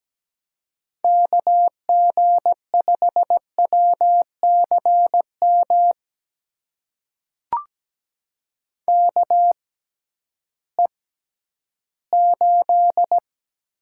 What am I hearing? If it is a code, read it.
Morse code, heard as a keyed tone, about 17 wpm, spaced as for 11 wpm, 700 Hz. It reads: KG5WCM KE8
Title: USA Call Signs - 6 Characters - 6x Character Spacing - 17wpm